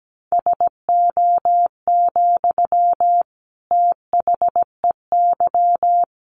Morse code, heard as a keyed tone, about 17 wpm, 700 Hz. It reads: SO, THEY
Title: Sentences from Top 1400 Words 17wpm